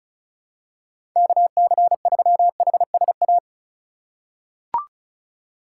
Morse code, heard as a keyed tone, about 34 wpm, 700 Hz. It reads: KC3HSA
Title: USA Call Signs - 6 Characters - 2x Character Spacing - 35wpm